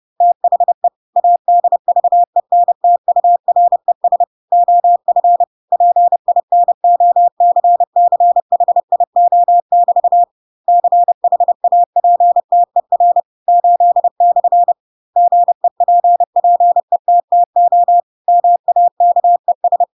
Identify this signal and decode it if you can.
Morse code, heard as a keyed tone, about 30 wpm, 700 Hz. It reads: THE ADVENTURES OF PINOCCHIO<BT> CHAPTER 8/ GEPPETTO MAKES
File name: Pinocchio / Pinocchio ch8 30 30 ik1hge